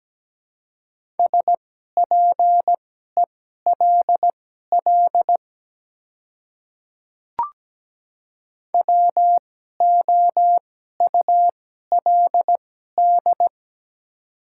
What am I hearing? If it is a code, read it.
Morse code, heard as a keyed tone, about 17 wpm, spaced as for 12 wpm, 700 Hz. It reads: SPELL WOULD
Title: 2x Character Spacing - Top 200 Words - 17wpm